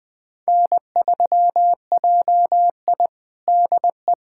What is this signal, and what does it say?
Morse code, heard as a keyed tone, about 21 wpm, 700 Hz. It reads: N3JI DE
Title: USA Callsign DE Callsign - 4 Characters - 20wpm